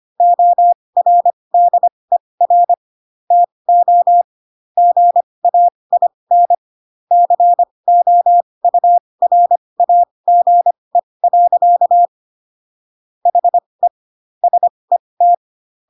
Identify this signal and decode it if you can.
Morse code, heard as a keyed tone, about 25 wpm, spaced as for 20 wpm, 700 Hz. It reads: ORDER TO GAIN COURAGE. HE SET
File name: Pinocchio / Pinocchio ch1 15 25 ik1hge